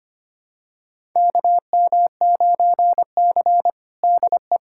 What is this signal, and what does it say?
Morse code, heard as a keyed tone, about 25 wpm, 700 Hz. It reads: KM9C DE
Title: USA Callsign DE Callsign - 4 Characters - 25wpm